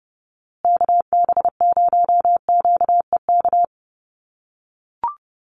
Morse code, heard as a keyed tone, about 30 wpm, 700 Hz. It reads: KB0QEK
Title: USA Call Signs Repeated 2x 30wpm